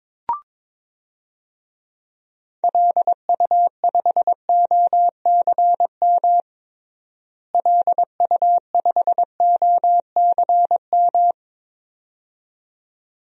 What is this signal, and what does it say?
Morse code, heard as a keyed tone, about 21 wpm, 700 Hz. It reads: LU5OCM LU5OCM
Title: DXCC-100 Call Signs - 6 Characters - Repeated 2x - 22wpm